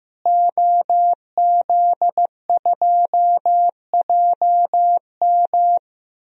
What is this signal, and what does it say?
Morse code, heard as a keyed tone, about 15 wpm, 700 Hz. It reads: OZ2JM
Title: DXCC-100 Call Signs - 5 Characters - Repeated 3x - 15wpm